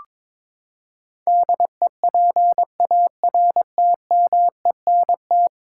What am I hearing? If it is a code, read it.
Morse code, heard as a keyed tone, about 21 wpm, 700 Hz. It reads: DEPARTMENT
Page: morse-code-ninja-practice-22wpm